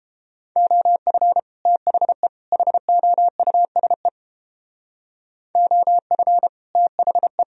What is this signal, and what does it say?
Morse code, heard as a keyed tone, about 31 wpm, 700 Hz. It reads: OF THE HOUSE OF THE